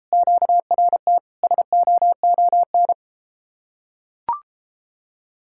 Morse code, heard as a keyed tone, about 33 wpm, 700 Hz. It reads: QRT SOON